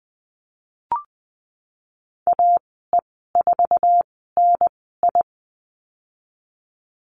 Morse code, heard as a keyed tone, about 20 wpm, spaced as for 15 wpm, 700 Hz. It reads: AE4NI
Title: 082 Lesson 14g - Number 2 - Call Signs 20wpm